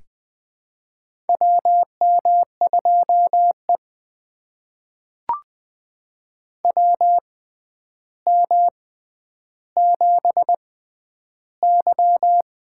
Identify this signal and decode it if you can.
Morse code, heard as a keyed tone, about 21 wpm, spaced as for 10 wpm, 700 Hz. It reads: WM2E WM7Y
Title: USA Call Signs - 4 Characters - 6x Character Spacing - 20wpm